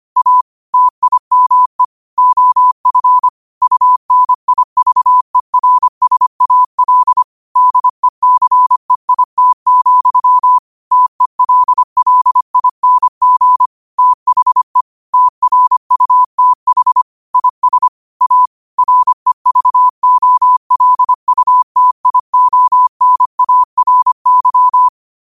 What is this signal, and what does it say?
Morse code, heard as a keyed tone, about 25 wpm, 1000 Hz. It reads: A TIME OF UNIVERSAL DECEIT, TELLING THE TRUTH IS A REVOLUTIONARY
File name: qotdfiles / 25wpm